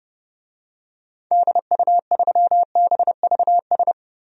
Morse code, heard as a keyed tone, about 30 wpm, 700 Hz. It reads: DU3BVS
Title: DXCC-100 Call Signs - 6 Characters - 30wpm